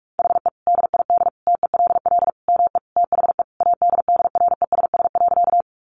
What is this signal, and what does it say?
Morse code, heard as a keyed tone, about 44 wpm, 700 Hz. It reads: HE DID TELL ME THE ADDRESS.
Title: Sentences from Top 1000 Words 45wpm